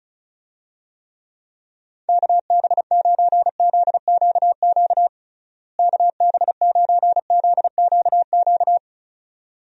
Morse code, heard as a keyed tone, about 36 wpm, 700 Hz. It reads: KB9ZQQ KB9ZQQ
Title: USA Call Signs - 6 Characters - Repeated 2x - 35wpm